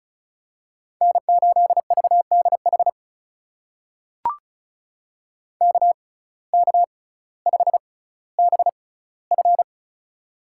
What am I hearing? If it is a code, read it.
Morse code, heard as a keyed tone, about 36 wpm, spaced as for 18 wpm, 700 Hz. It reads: N8VDH KK5BF
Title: USA Call Signs - 6x Character Spacing - 35wpm